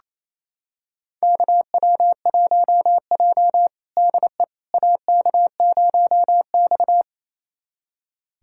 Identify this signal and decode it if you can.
Morse code, heard as a keyed tone, about 28 wpm, 700 Hz. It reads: KW1J DE AK0X
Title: USA Callsign DE Callsign - 4 Characters - 28wpm